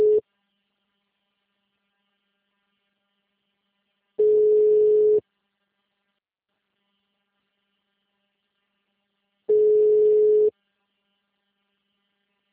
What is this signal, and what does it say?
Ringback tone (Europe, 425 Hz).